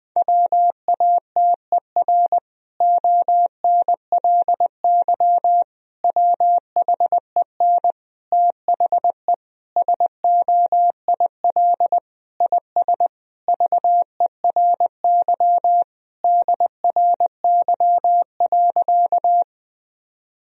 Morse code, heard as a keyed tone, about 20 wpm, 700 Hz. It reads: WATER ONLY WHEN THE SOIL IS VERY DRY.
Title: Sentences from Top 1000 Words - Review Part 4 - 20wpm